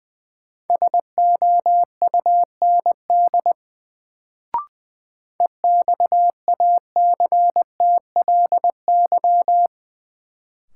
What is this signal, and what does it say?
Morse code, heard as a keyed tone, about 20 wpm, 700 Hz. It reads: SOUND EXACTLY
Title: Sentences from Top 900 Words - New Word Drill - 20wpm